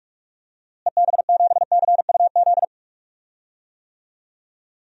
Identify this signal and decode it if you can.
Morse code, heard as a keyed tone, about 41 wpm, 700 Hz. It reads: EB7CUZ
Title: DXCC-100 Call Signs - 6 Characters - Repeated 2x - 45wpm